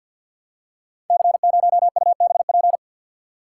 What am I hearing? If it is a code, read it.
Morse code, heard as a keyed tone, about 47 wpm, 700 Hz. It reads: C0UBP
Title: Code Groups - 5 Letter:Numbers - 50wpm